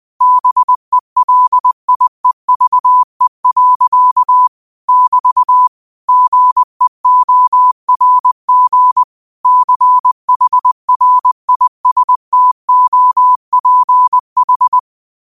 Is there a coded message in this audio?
Yes: Morse code from 0.2 to 14.8 s, heard as a keyed tone, about 20 wpm, 1000 Hz. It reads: BELIEVE. <BT> GEORG CHRISTOPH